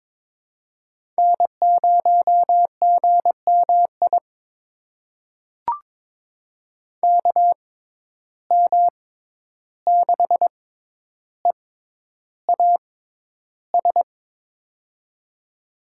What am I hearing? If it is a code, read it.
Morse code, heard as a keyed tone, about 22 wpm, spaced as for 11 wpm, 700 Hz. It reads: N0GMI KM6EAS
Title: USA Call Signs - 6x Character Spacing - 22wpm